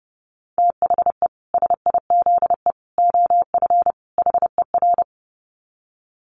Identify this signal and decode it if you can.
Morse code, heard as a keyed tone, about 30 wpm, 700 Hz. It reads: THE SIZE OF HER
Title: Dit Challenge 30wpm